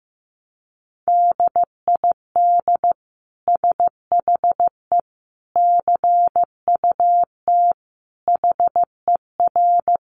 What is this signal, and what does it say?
Morse code, heard as a keyed tone, about 15 wpm, 700 Hz. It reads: DID SHE CUT HER
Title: Sentences from Top 1300 Words 15wpm